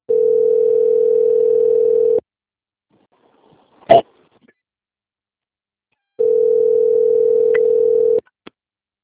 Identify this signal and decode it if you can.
Ringback tone (North America), 440 + 480 Hz.